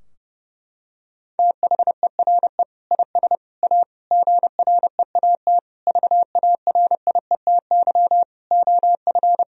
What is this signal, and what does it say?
Morse code, heard as a keyed tone, about 32 wpm, 700 Hz. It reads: THERE IS A GREAT VARIETY OF
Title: Sentences from Top 1000 Words - Review Part 4 - 30wpm